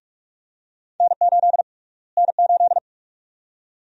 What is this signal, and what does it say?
Morse code, heard as a keyed tone, about 46 wpm, 700 Hz. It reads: N8 N8
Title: USA Call Sign Prefixes - Repeated 2x - 45wpm